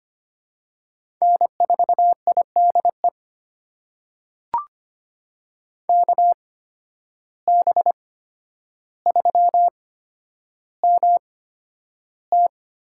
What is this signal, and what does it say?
Morse code, heard as a keyed tone, about 25 wpm, spaced as for 11 wpm, 700 Hz. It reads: N4IDE KB3MT
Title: USA Call Signs - 5 Characters - 8x Character Spacing - 25wpm